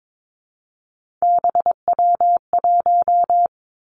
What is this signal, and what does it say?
Morse code, heard as a keyed tone, about 22 wpm, 700 Hz. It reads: BW1
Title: Three Challenging Letter-Number Combinations 22wpm